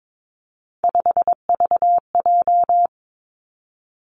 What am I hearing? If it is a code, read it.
Morse code, heard as a keyed tone, about 22 wpm, 700 Hz. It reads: 5VJ
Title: Three Challenging Letter-Number Combinations 22wpm